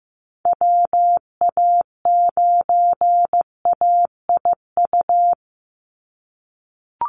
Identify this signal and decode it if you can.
Morse code, heard as a keyed tone, about 15 wpm, 700 Hz. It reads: WA9AIU
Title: USA Call Signs 15wpm